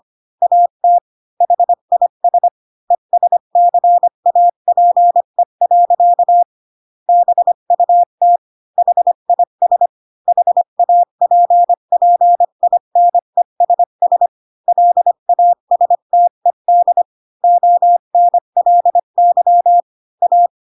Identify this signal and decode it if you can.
Morse code, heard as a keyed tone, about 25 wpm, 700 Hz. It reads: AT HIS ESCAPE. BUT HIS HAPPINESS LASTED ONLY A
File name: Pinocchio / Pinocchio ch4 20 25 ik1hge